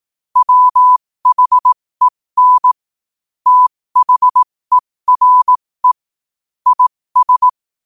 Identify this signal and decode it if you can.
Morse code, heard as a keyed tone, about 18 wpm, spaced as for 15 wpm, 1000 Hz. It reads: WHEN THERE IS
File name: qotdfiles / 15wpm